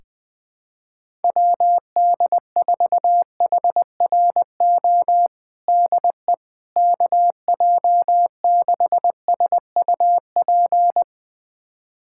Morse code, heard as a keyed tone, about 20 wpm, 700 Hz. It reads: WD4HRO DE KJ6SUP
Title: USA Callsign DE Callsign 20wpm